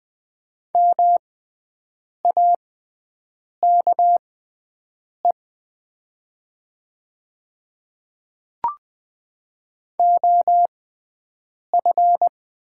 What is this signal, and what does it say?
Morse code, heard as a keyed tone, about 20 wpm, spaced as for 7 wpm, 700 Hz. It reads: MAKE OF
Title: 6x Character Spacing - Top 200 Words - 20wpm